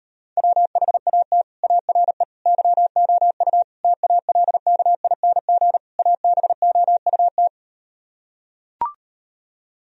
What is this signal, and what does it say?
Morse code, heard as a keyed tone, about 38 wpm, 700 Hz. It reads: WHAT ARE YOU TALKING ABOUT